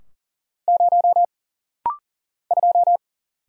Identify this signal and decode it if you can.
Morse code, heard as a keyed tone, about 40 wpm, 700 Hz. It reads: 0 2
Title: Single Letter-Number - Rapid-Fire 40wpm